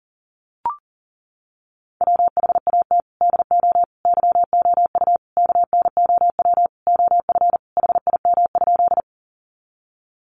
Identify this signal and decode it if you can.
Morse code, heard as a keyed tone, about 40 wpm, 700 Hz. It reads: WHAT DO YOU KNOW OF HIM?
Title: Sentences from Top 100 Words 40wpm